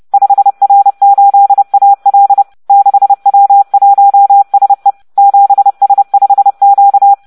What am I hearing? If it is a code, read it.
Morse code, heard as a keyed tone, about 30 wpm, 800 Hz. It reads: 5R8AL 6W1SE 7S5Q